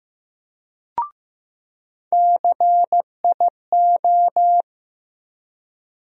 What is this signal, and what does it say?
Morse code, heard as a keyed tone, about 15 wpm, 700 Hz. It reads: CIO
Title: USA Call Sign Postfixes - 3 Letters 15wpm